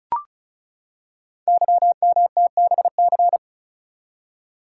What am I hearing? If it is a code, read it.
Morse code, heard as a keyed tone, about 36 wpm, 690 Hz. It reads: YMTBC
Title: Code Groups - 5 Letters - 35wpm